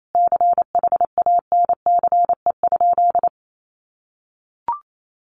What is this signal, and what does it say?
Morse code, heard as a keyed tone, about 28 wpm, 700 Hz. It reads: CHANCE?
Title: Sentences from Top 800 Words 28wpm